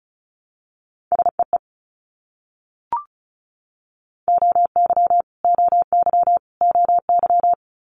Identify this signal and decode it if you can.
Morse code, heard as a keyed tone, about 34 wpm, 700 Hz. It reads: SEE OY OY OY
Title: Multi-letter Phonemes Repeated 3x 35wpm